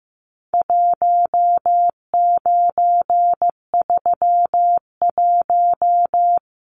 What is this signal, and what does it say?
Morse code, heard as a keyed tone, about 15 wpm, 700 Hz. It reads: 1931
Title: Numbers - 4-Digit 15wpm